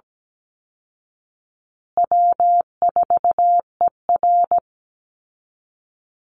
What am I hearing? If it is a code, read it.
Morse code, heard as a keyed tone, about 17 wpm, 700 Hz. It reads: W4ER
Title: Easy Call Signs Repeated 2x - 17wpm